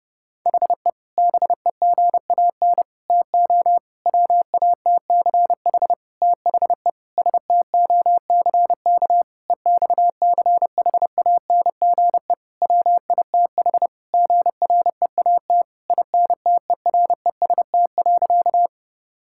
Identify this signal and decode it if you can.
Morse code, heard as a keyed tone, about 30 wpm, 700 Hz. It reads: HE BEGAN TO WATCH THE STOCK EXCHANGE WITH GREAT INTEREST.
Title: Sentences from Top 1000 Words - Review Part 6 - 30wpm